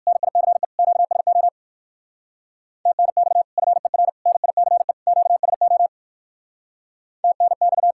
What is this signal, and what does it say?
Morse code, heard as a keyed tone, about 58 wpm, 700 Hz. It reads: NICE QSO TNX FER NICE QSO TNX